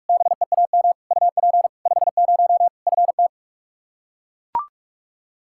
Morse code, heard as a keyed tone, about 41 wpm, 700 Hz. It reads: BEAM UP 50 FT
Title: Practice QSO - Exchange 2 - Info - 45wpm